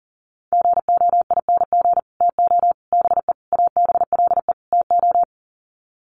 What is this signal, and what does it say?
Morse code, heard as a keyed tone, about 40 wpm, 700 Hz. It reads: GOING TO BE ABLE TO
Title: Sets of 5 Words 40wpm